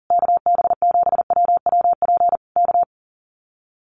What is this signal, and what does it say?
Morse code, heard as a keyed tone, about 37 wpm, 700 Hz. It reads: KB7WWP K
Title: Common QSO Phrases Encore 40wpm